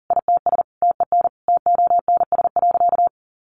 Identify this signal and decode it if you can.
Morse code, heard as a keyed tone, about 40 wpm, 700 Hz. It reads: ITS TEN TONS.
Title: A Sound Of Thunder 40wpm